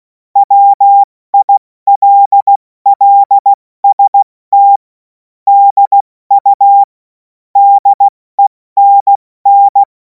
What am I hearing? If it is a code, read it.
Morse code, heard as a keyed tone, about 16 wpm, spaced as for 14 wpm, 800 Hz. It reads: WILLST DU DENN
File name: Morsekurs / Witze / Witz 03. März 2026 70 F80